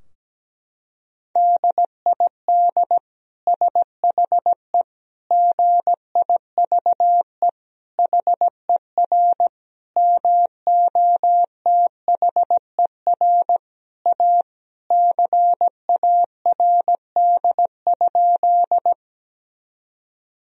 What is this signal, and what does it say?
Morse code, heard as a keyed tone, about 17 wpm, 700 Hz. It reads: DID SHE GIVE HER MOTHER A CARD?
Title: Sentences from Top 1000 Words - Review Part 4 - 17wpm